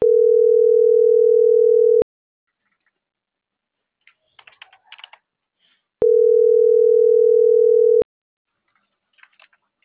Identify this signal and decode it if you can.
Ringback tone (North America), 440 + 480 Hz.